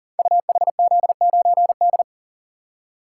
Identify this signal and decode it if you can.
Morse code, heard as a keyed tone, about 39 wpm, 700 Hz. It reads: UHZ9D
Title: Code Groups - 5 Letter:Numbers - 40wpm